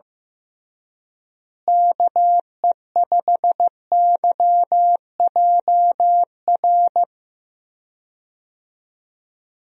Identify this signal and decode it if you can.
Morse code, heard as a keyed tone, about 15 wpm, 700 Hz. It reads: KE5YJR
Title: USA Call Signs - 6 Characters - Repeated 3x - 15wpm